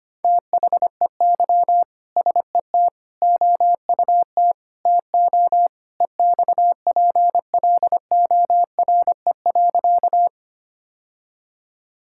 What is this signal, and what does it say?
Morse code, heard as a keyed tone, about 25 wpm, 700 Hz. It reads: THEY SET OUT TO EXPLORE.
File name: practice / 25wpm / Sentences with 2-Word Phrases - 25wpm